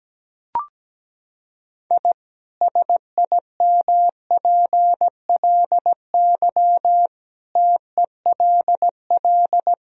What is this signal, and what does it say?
Morse code, heard as a keyed tone, about 17 wpm, 700 Hz. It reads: I SIMPLY TELL
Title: Sentences from Top 1000 Words - Review Part 4 - 17wpm